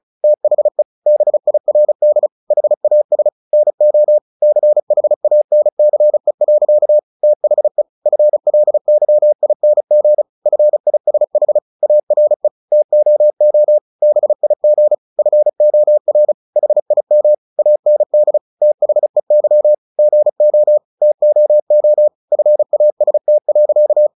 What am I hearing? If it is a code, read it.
Morse code, heard as a keyed tone, about 34 wpm, 600 Hz. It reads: THE BIRD HAS NO CHANCE. THE FLYING FISH ARE TOO BIG FOR HIM AND THEY GO TOO FAST.
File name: the old man and the sea / the old man and the sea 1 35wpm